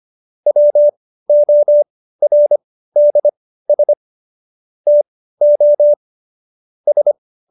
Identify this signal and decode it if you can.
Morse code, heard as a keyed tone, about 25 wpm, spaced as for 15 wpm, 590 Hz. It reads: WORDS TO S
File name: Jokes-25-char-15-eff